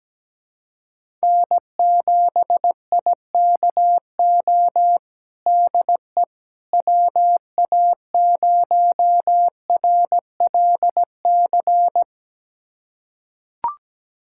Morse code, heard as a keyed tone, about 17 wpm, 700 Hz. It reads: N7IKO DE WA0RLC
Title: USA Callsign DE Callsign 17wpm